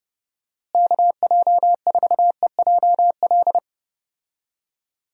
Morse code, heard as a keyed tone, about 31 wpm, 700 Hz. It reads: KJ4EJL
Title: USA Call Signs - 6 Characters - Repeated 2x - 30wpm